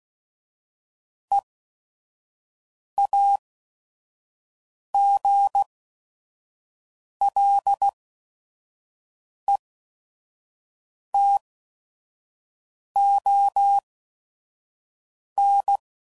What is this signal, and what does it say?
Morse code, heard as a keyed tone, about 16 wpm, spaced as for 5 wpm, 800 Hz. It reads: EAGLETON
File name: CW Code Groups @5WPM / fromSM's / Cities